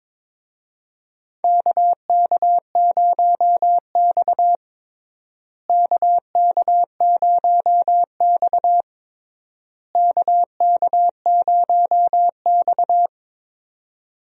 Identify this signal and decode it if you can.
Morse code, heard as a keyed tone, about 22 wpm, 700 Hz. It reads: KK0X KK0X KK0X
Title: USA Call Signs - 4 Characters - Repeated 3x - 22wpm